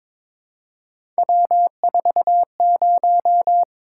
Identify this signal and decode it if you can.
Morse code, heard as a keyed tone, about 22 wpm, 700 Hz. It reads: W40
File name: practice / 22wpm / Three Letter-Number Combinations - 22wpm